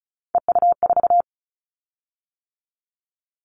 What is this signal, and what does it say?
Morse code, heard as a keyed tone, about 34 wpm, 700 Hz. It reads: EU4
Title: Callsign Prefixes for Top 100 DXCC - 35wpm